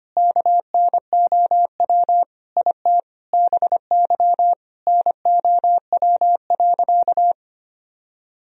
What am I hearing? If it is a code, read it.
Morse code, heard as a keyed tone, about 25 wpm, 700 Hz. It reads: KNOW IT BY NOW.
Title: Sentences from Top 1000 Words - Review Part 2 - 25wpm